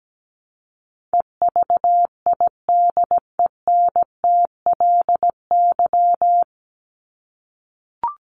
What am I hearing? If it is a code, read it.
Morse code, heard as a keyed tone, about 17 wpm, 700 Hz. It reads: EVIDENTLY
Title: LY Suffix - Top 100 Words with Base Words 17wpm